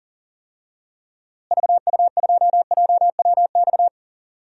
Morse code, heard as a keyed tone, about 40 wpm, 700 Hz. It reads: VU2JWX
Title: DXCC-100 Call Signs - 6 Characters - Repeated 2x - 40wpm